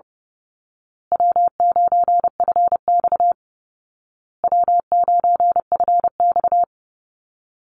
Morse code, heard as a keyed tone, about 30 wpm, 700 Hz. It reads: W9FX W9FX
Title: Easy Call Signs Repeated 3x - 30wpm